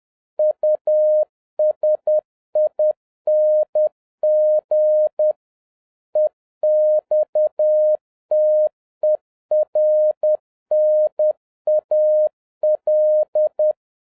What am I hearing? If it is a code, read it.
Morse code, heard as a keyed tone, about 10 wpm, 610 Hz. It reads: USING EXTERNAL
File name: 2011-07-mp3 / n36 THE 3.11 2 10wpm